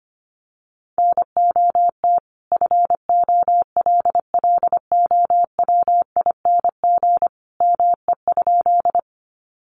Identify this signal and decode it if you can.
Morse code, heard as a keyed tone, about 25 wpm, 700 Hz. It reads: NOT FOLLOWING ME?
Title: Sentences from Top 700 Words 25wpm